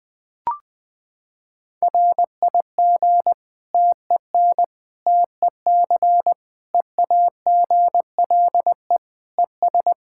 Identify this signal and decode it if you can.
Morse code, heard as a keyed tone, about 20 wpm, 700 Hz. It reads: RIG TEN TEC EAGLE ES
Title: K1USN SST QSO Practice 20wpm